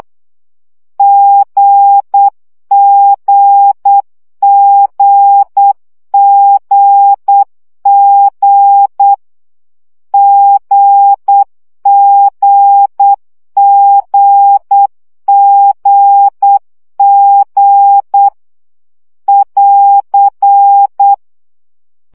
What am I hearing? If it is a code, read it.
Morse code, heard as a keyed tone, about 8 wpm, 800 Hz. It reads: GGGGG GGGGG <AR>
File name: CW gyakorlas g 7 baud